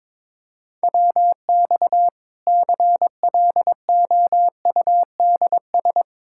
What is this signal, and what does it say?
Morse code, heard as a keyed tone, about 22 wpm, 700 Hz. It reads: WX CLOUDS
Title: Practice QSO - Exchange 2 - Info - 22wpm